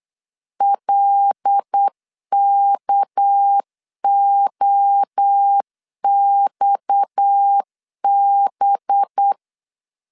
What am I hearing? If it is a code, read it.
Morse code, heard as a keyed tone, about 8 wpm, 800 Hz. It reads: LKOXB
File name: LEZIONI / 6 PKX0 / 6 PKX 8wpm 3041